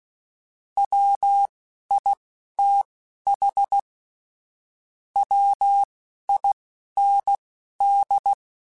Morse code, heard as a keyed tone, about 16 wpm, spaced as for 12 wpm, 800 Hz. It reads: WITH WIND
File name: Weather Sets / Weather Sets-mp3s / Weather 4 28 @10